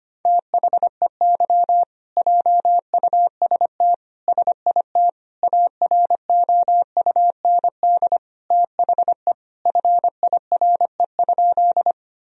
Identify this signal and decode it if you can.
Morse code, heard as a keyed tone, about 25 wpm, 700 Hz. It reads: THEY JUST SIT AROUND THE FIRE?
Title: Sentences from Top 1000 Words - Review Part 3 - 25wpm